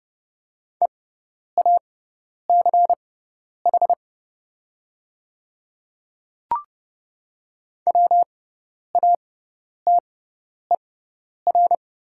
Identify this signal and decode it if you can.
Morse code, heard as a keyed tone, about 30 wpm, spaced as for 10 wpm, 700 Hz. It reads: EACH WATER
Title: 6x Character Spacing - Top 100 Words - 30wpm